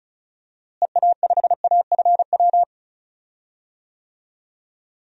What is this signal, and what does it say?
Morse code, heard as a keyed tone, about 34 wpm, 700 Hz. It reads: EA5AFW
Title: DXCC-100 Call Signs - 6 Characters - Repeated 2x - 35wpm